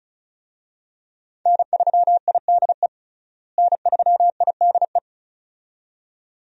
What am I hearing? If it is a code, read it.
Morse code, heard as a keyed tone, about 36 wpm, 700 Hz. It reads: N3IDE N3IDE
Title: USA Call Signs - 5 Characters - Repeated 2x - 35wpm